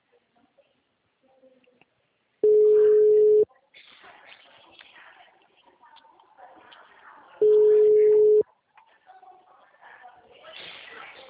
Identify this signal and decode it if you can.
Ringback tone (Europe, 425 Hz), with speech.